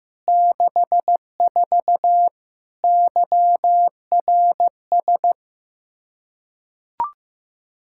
Morse code, heard as a keyed tone, about 15 wpm, 700 Hz. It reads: 64 YRS
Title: K1USN SST QSO Practice 15wpm